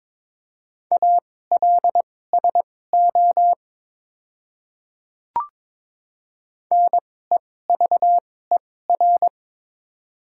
Morse code, heard as a keyed tone, about 22 wpm, spaced as for 16 wpm, 700 Hz. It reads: ALSO NEVER
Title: 2x Character Spacing - Top 500 Words - 22wpm